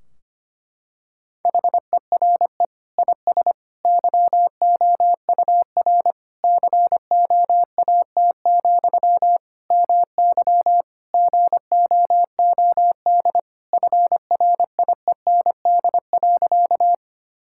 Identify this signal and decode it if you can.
Morse code, heard as a keyed tone, about 25 wpm, 700 Hz. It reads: HERE IS YOUR COAT, MY GOOD FRIEND.
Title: Sentences from Top 1600 Words 25wpm